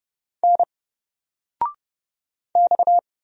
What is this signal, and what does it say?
Morse code, heard as a keyed tone, about 30 wpm, 700 Hz. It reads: N X
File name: Single-Letter-Number-30wpm